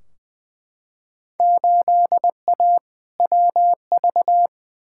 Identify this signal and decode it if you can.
Morse code, heard as a keyed tone, about 20 wpm, 700 Hz. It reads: 8A WV
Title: Field Day Exchanges - 20wpm